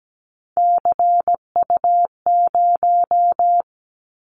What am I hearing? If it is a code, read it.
Morse code, heard as a keyed tone, about 17 wpm, 700 Hz. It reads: CU0
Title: Callsign Prefixes for Top 100 DXCC - 17wpm